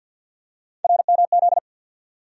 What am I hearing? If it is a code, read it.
Morse code, heard as a keyed tone, about 48 wpm, 700 Hz. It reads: RMZ
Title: USA Call Sign Postfixes - 3 Letters 50wpm